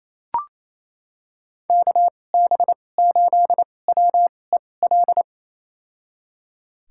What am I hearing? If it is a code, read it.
Morse code, heard as a keyed tone, about 28 wpm, spaced as for 20 wpm, 700 Hz. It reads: KB8WEL
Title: USA Call Signs - 6 Characters - 2x Character Spacing - 28wpm